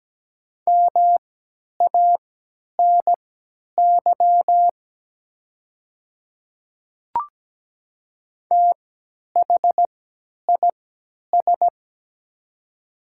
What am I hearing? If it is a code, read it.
Morse code, heard as a keyed tone, about 17 wpm, spaced as for 10 wpm, 700 Hz. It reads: MANY THIS
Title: 3x Character Spacing - Top 100 Words - 17wpm